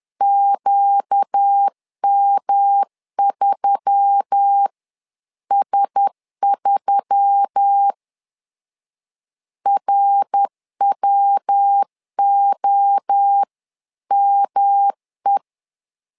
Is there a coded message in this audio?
Yes: Morse code from 0.2 to 15.4 s, heard as a keyed tone, about 10 wpm, 800 Hz. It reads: QM3S3 RWOME